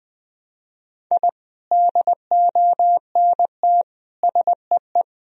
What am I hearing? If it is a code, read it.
Morse code, heard as a keyed tone, about 21 wpm, 700 Hz. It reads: I DONT SEE
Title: Sentences from Top 1000 Words - Review Part 4 - 20wpm